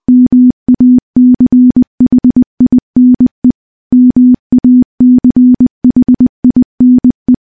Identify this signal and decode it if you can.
Morse code, heard as a keyed tone, about 20 wpm, 250 Hz. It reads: MACHINE MACHINE
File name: morse-training-20WPM